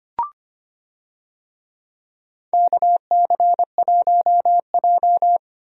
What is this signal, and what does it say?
Morse code, heard as a keyed tone, about 25 wpm, 700 Hz. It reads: KC1J
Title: USA Call Signs - 4 Characters - 25wpm